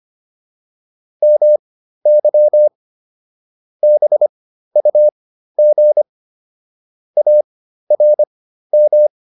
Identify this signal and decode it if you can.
Morse code, heard as a keyed tone, about 25 wpm, spaced as for 13 wpm, 600 Hz. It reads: MY BUG ARM
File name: PR202 13